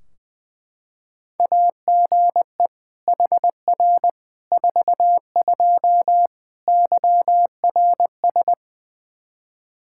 Morse code, heard as a keyed tone, about 20 wpm, 700 Hz. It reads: AGE HR 42 YRS
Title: Practice QSO - Exchange 3 - More Info - 20wpm